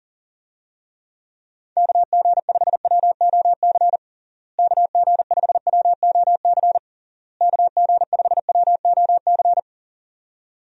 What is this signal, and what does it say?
Morse code, heard as a keyed tone, about 40 wpm, 700 Hz. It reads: KG5WOC KG5WOC KG5WOC
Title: USA Call Signs - 6 Characters - Repeated 3x - 40wpm